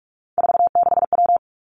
Morse code, heard as a keyed tone, about 46 wpm, 700 Hz. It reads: 46W
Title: Three Challenging Letter-Number Combinations 45wpm